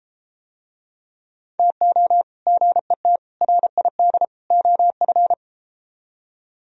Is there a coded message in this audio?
Yes: Morse code from 1.6 to 5.3 s, heard as a keyed tone, about 31 wpm, 700 Hz. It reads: TO GET RID OF